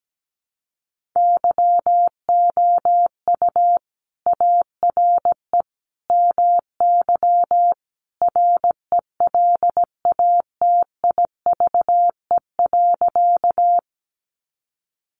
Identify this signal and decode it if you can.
Morse code, heard as a keyed tone, about 17 wpm, 700 Hz. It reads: YOU ARE MY RELATIVE.
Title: Sentences from Top 1400 Words 17wpm